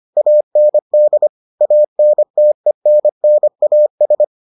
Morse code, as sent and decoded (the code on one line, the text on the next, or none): .- -. -.. / .- -. - . -. -. .- ...
AND ANTENNAS